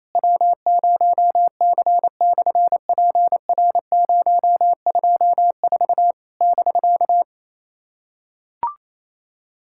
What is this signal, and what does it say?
Morse code, heard as a keyed tone, about 28 wpm, 700 Hz. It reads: W0C/PR024 <BK>
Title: SOTA QSO Practice 28wpm